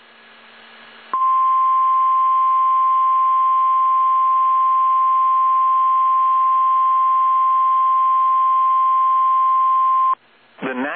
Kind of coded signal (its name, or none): Emergency Alert System (SAME) tones